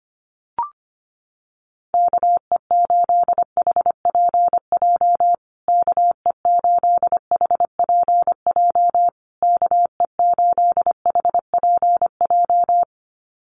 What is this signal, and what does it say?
Morse code, heard as a keyed tone, about 25 wpm, 700 Hz. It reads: KE8HPJ KE8HPJ KE8HPJ
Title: USA Call Signs Repeated 3x 25wpm